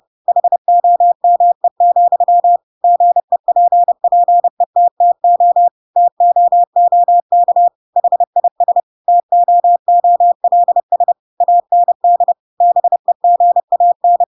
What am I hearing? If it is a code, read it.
Morse code, heard as a keyed tone, about 30 wpm, 700 Hz. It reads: HOME, GEPPETTO TOOK HIS TOOLS AND BEGAN
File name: Pinocchio / Pinocchio ch3 30 30 ik1hge